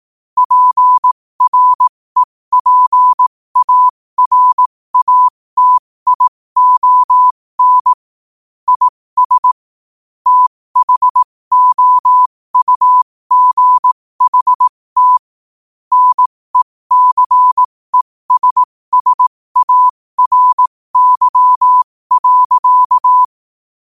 Morse code, heard as a keyed tone, about 18 wpm, spaced as for 15 wpm, 1000 Hz. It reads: PREPARATION IS THOUGHT NECESSARY.
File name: qotdfiles / 15wpm